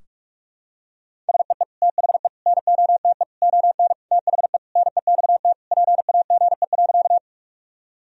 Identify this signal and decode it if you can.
Morse code, heard as a keyed tone, about 46 wpm, 700 Hz. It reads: SEE THE NOTE ON THE NEXT PAGE.
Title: Sentences from Top 1000 Words - Review Part 2 - 45wpm